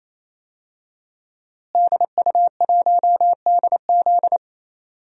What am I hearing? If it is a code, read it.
Morse code, heard as a keyed tone, about 28 wpm, 700 Hz. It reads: DU1DZ
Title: DXCC-100 Call Signs - 5 Characters - Repeated 3x - 28wpm